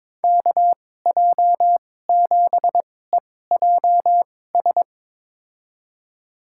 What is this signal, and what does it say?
Morse code, heard as a keyed tone, about 22 wpm, spaced as for 16 wpm, 700 Hz. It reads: KJ7EJS
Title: USA Call Signs - 6 Characters - 2x Character Spacing - 22wpm